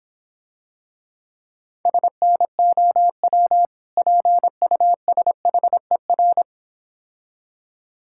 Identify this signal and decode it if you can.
Morse code, heard as a keyed tone, about 26 wpm, 700 Hz. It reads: SNOW PUSHER